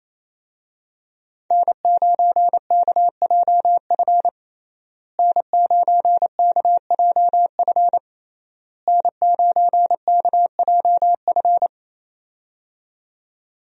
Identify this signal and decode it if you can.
Morse code, heard as a keyed tone, about 28 wpm, 700 Hz. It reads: N9KJF N9KJF N9KJF
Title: USA Call Signs - 5 Characters - Repeated 3x - 28wpm